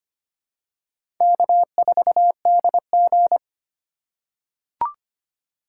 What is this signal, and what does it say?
Morse code, heard as a keyed tone, about 25 wpm, 700 Hz. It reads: K4DG
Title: USA Call Signs - 4x Character Spacing - 25wpm